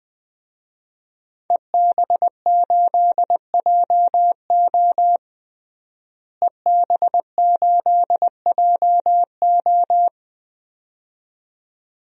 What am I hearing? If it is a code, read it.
Morse code, heard as a keyed tone, about 20 wpm, 700 Hz. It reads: EB8JO EB8JO
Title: DXCC-100 Call Signs - 5 Characters - Repeated 2x - 20wpm